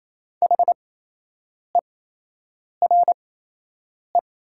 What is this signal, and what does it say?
Morse code, heard as a keyed tone, about 28 wpm, spaced as for 8 wpm, 700 Hz. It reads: HERE
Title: 8x Character Spacing - Top 200 Words - 28wpm